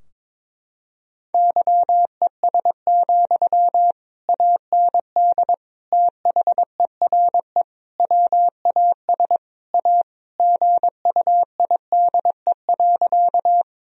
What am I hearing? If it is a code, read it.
Morse code, heard as a keyed tone, about 22 wpm, 700 Hz. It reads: YES, AND THERE WAS A GUIDE.
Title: Sentences from Top 1000 Words - Review Part 3 - 22wpm